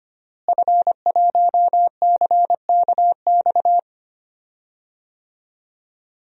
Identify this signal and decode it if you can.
Morse code, heard as a keyed tone, about 25 wpm, 700 Hz. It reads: F1CKX
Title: DXCC-100 Call Signs - 5 Characters - Repeated 2x - 25wpm